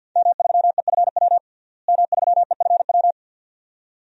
Morse code, heard as a keyed tone, about 52 wpm, 700 Hz. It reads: M3EFW M3EFW
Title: DXCC-100 Call Signs - 5 Characters - Repeated 2x - 50wpm